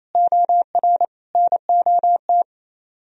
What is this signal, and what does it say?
Morse code, heard as a keyed tone, about 35 wpm, 700 Hz. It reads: OR NOT
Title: Speed-Racing - Sets of 3 Words 28wpm